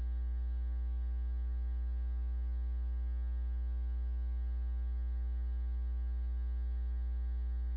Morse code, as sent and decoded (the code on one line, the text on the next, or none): none